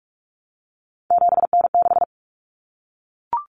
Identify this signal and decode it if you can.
Morse code, heard as a keyed tone, about 44 wpm, 700 Hz. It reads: 7N6
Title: Common Cut Numbers - 3-Digit 45wpm